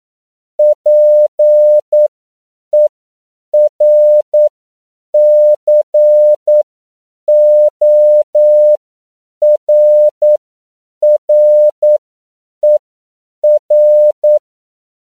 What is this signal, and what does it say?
Morse code, heard as a keyed tone, about 9 wpm, spaced as for 7 wpm, 600 Hz. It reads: PERCORRER